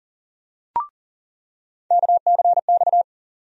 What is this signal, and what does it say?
Morse code, heard as a keyed tone, about 38 wpm, 700 Hz. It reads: KCX
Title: USA Call Sign Postfixes - 3 Letters 40wpm